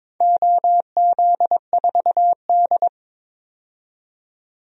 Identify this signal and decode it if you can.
Morse code, heard as a keyed tone, about 22 wpm, 700 Hz. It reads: OZ4D
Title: DXCC-100 Call Signs - 4 Characters - 22wpm